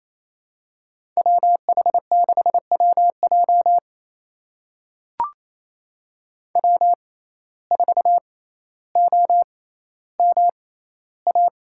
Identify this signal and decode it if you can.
Morse code, heard as a keyed tone, about 28 wpm, spaced as for 14 wpm, 700 Hz. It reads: WH6WJ W4OMA
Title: USA Call Signs - 5 Characters - 6x Character Spacing - 28wpm